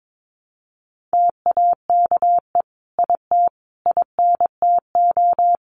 Morse code, heard as a keyed tone, about 22 wpm, 700 Hz. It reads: TAKE IT INTO
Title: Sentences from Top 1500 Words 22wpm